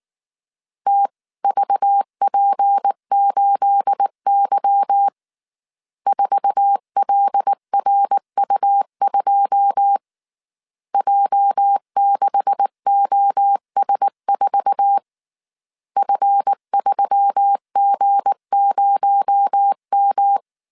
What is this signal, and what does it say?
Morse code, heard as a keyed tone, about 19 wpm, 800 Hz. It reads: TVP8Y 4LRU2 J6OS4 F3G0M
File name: CW / DET / 1 9 1500C S A 18wpm 2459